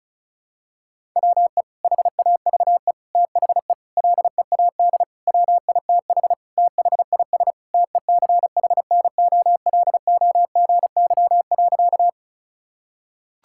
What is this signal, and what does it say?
Morse code, heard as a keyed tone, about 36 wpm, 700 Hz. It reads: WE HAVE THE LEAD WITH THIS TECHNOLOGY.
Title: Sentences from Top 1000 Words - Review Part 5 - 35wpm